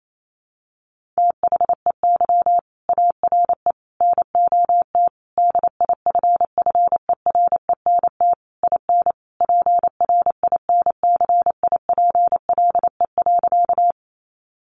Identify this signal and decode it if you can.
Morse code, heard as a keyed tone, about 28 wpm, 700 Hz. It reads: THEY ARE NOT DIFFERENT IN PRINCIPLE.
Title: Sentences from Top 1400 Words 28wpm